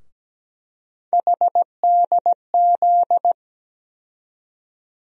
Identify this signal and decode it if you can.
Morse code, heard as a keyed tone, about 17 wpm, 700 Hz. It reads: HDZ
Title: USA Call Sign Postfixes - 3 Letters 17wpm